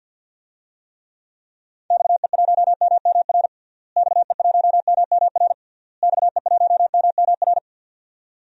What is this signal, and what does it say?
Morse code, heard as a keyed tone, about 48 wpm, 700 Hz. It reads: XE1MMR XE1MMR XE1MMR
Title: DXCC-100 Call Signs - 6 Characters - Repeated 3x - 50wpm